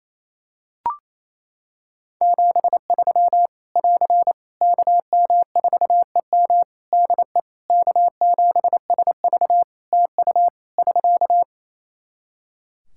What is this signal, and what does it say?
Morse code, heard as a keyed tone, about 27 wpm, 700 Hz. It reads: 73 <AR> KM4EM DE K7SV TU <SK>
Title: Practice QSO - Exchange 4 - Ending - 28wpm